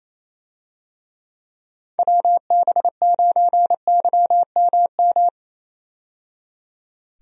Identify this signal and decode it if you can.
Morse code, heard as a keyed tone, about 27 wpm, 700 Hz. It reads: WB9YMM
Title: USA Call Signs - 6 Characters - 28wpm